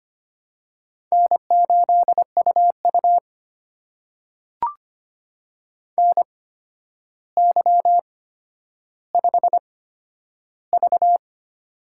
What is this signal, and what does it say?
Morse code, heard as a keyed tone, about 25 wpm, spaced as for 11 wpm, 700 Hz. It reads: N8UU NY5V
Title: USA Call Signs - 4 Characters - 8x Character Spacing - 25wpm